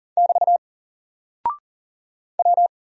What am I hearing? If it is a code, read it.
Morse code, heard as a keyed tone, about 40 wpm, 700 Hz. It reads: <BT> W
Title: Single Character (Letter:Number:Punctuation:Prosign) Alt 40wpm